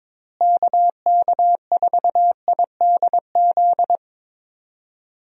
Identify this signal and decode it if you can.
Morse code, heard as a keyed tone, about 22 wpm, 700 Hz. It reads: KK4IDZ
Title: USA Call Signs - 6 Characters - Repeated 2x - 22wpm